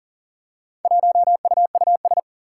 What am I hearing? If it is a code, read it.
Morse code, heard as a keyed tone, about 38 wpm, 700 Hz. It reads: 1UUS
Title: Four Challenging Letter-Number Combinations 40wpm